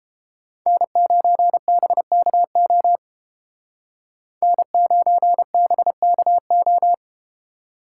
Morse code, heard as a keyed tone, about 31 wpm, 700 Hz. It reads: N9BKO N9BKO